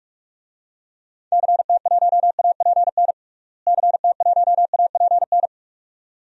Morse code, heard as a keyed tone, about 46 wpm, 700 Hz. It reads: CT1APN CT1APN
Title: DXCC-100 Call Signs - 6 Characters - Repeated 2x - 45wpm